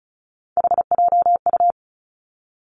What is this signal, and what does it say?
Morse code, heard as a keyed tone, about 36 wpm, 700 Hz. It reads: HJU
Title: Three Challenging Letter-Number Combinations 35wpm